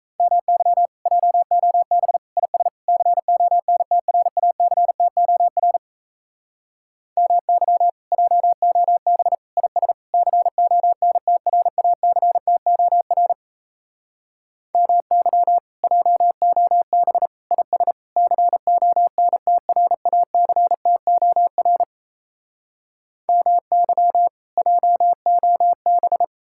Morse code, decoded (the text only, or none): MY JOB IS CONTRACTOR MY JOB IS CONTRACTOR MY JOB IS CONTRACTOR MY JOB